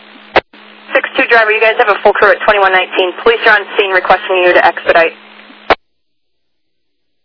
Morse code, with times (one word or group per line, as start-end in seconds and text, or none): none